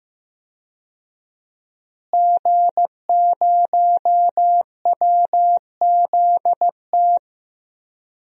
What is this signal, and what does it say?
Morse code, heard as a keyed tone, about 15 wpm, 700 Hz. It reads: G0WZT
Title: DXCC-100 Call Signs - 5 Characters - Repeated 2x - 15wpm